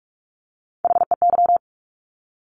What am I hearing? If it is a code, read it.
Morse code, heard as a keyed tone, about 46 wpm, 700 Hz. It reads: HEY
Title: Top Three Letter Combinations in Words 45wpm